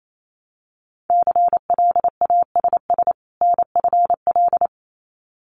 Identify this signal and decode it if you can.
Morse code, heard as a keyed tone, about 28 wpm, 700 Hz. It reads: CLASS NFL
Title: Winter Field Day Exchanges 28wpm